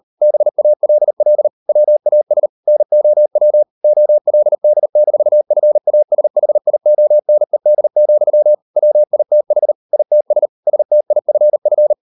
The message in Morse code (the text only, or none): BALL WAS NOW OLD-FASHIONED, WITH ITS STIFF